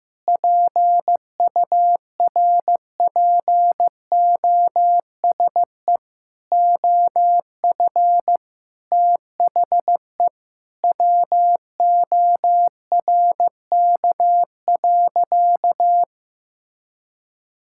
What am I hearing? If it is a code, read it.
Morse code, heard as a keyed tone, about 15 wpm, 700 Hz. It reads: PURPOSE OF THE WORK.
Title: Sentences from Top 1000 Words - Review Part 6 - 15wpm